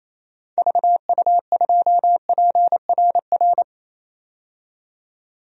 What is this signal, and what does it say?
Morse code, heard as a keyed tone, about 27 wpm, 700 Hz. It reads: VU2PRR
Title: DXCC-100 Call Signs - 6 Characters - 28wpm